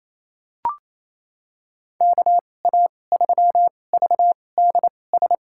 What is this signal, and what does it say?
Morse code, heard as a keyed tone, about 28 wpm, spaced as for 20 wpm, 700 Hz. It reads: KA3VDS
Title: USA Call Signs - 2x Character Spacing - 28wpm